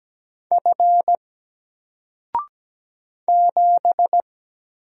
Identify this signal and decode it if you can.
Morse code, heard as a keyed tone, about 17 wpm, 700 Hz. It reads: F 7
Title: Single Character (Letter:Number:Punctuation:Prosign) 17wpm